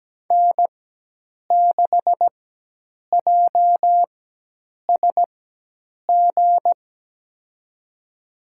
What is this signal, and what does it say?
Morse code, heard as a keyed tone, about 17 wpm, spaced as for 8 wpm, 700 Hz. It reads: N6JSG
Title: USA Call Signs - 4x Character Spacing - 17wpm